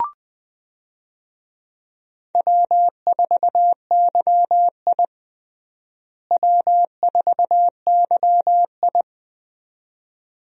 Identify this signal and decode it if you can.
Morse code, heard as a keyed tone, about 20 wpm, 700 Hz. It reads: W4YI W4YI
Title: USA Call Signs - 4 Characters - Repeated 2x - 20wpm